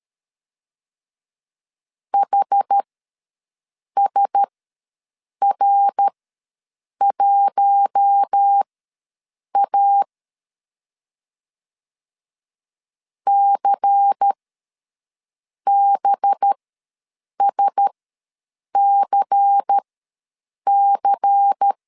Morse code, heard as a keyed tone, about 13 wpm, spaced as for 6 wpm, 800 Hz. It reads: HSR1A CBSCC
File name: LEZIONI / 5 LBVC / 5 LBVC ws 12wpm 3159